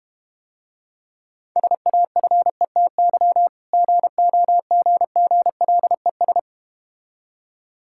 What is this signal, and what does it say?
Morse code, heard as a keyed tone, about 32 wpm, 700 Hz. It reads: SAFETY GOGGLES